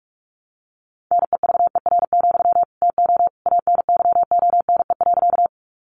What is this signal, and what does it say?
Morse code, heard as a keyed tone, about 44 wpm, 700 Hz. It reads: NEVER, TO ANYONE.
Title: Sentences from Top 900 Words 45wpm